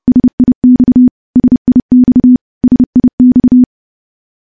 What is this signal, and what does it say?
Morse code, heard as a keyed tone, about 30 wpm, 250 Hz. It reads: SIX SIX SIX
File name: morse / morse-training-30WPM